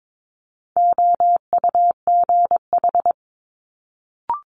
Morse code, heard as a keyed tone, about 22 wpm, 700 Hz. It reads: OUGH
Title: Multi-letter Phonemes Quick Fire 22wpm